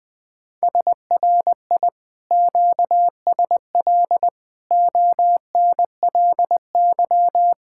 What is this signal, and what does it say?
Morse code, heard as a keyed tone, about 20 wpm, 700 Hz. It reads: SRI QSL ONLY
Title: Speed-Racing Common QSO Phrases Encore - 17wpm